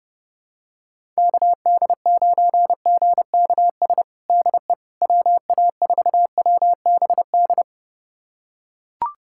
Morse code, heard as a keyed tone, about 30 wpm, 700 Hz. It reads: KD9GKS DE WA4WBD
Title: USA Callsign DE Callsign - 6 Characters - 30wpm